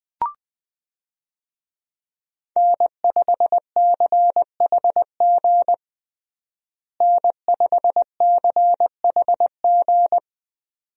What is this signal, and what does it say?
Morse code, heard as a keyed tone, about 21 wpm, 700 Hz. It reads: N5CHG N5CHG
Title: USA Call Signs - 5 Characters - Repeated 2x - 20wpm